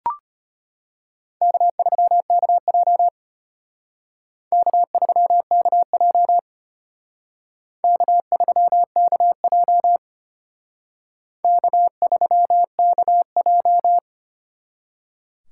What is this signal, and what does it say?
Morse code, heard as a keyed tone, about 32 wpm, 700 Hz. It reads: K3KJ K3KJ K3KJ K3KJ